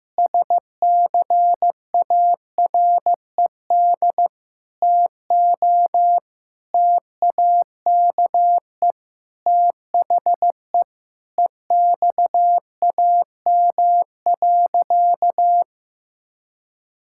Morse code, heard as a keyed tone, about 16 wpm, 700 Hz. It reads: SCARED TO TAKE THE EXAM.
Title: Sentences from Top 1000 Words - Review Part 5 - 15wpm